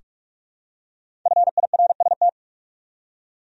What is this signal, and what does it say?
Morse code, heard as a keyed tone, about 46 wpm, 700 Hz. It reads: FIRST
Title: Top 100 Words 45wpm